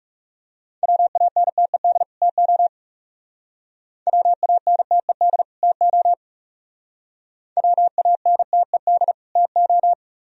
Morse code, heard as a keyed tone, about 37 wpm, 700 Hz. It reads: WANTED TO WANTED TO WANTED TO
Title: Speed-Racing - Sets of 2 Words 30wpm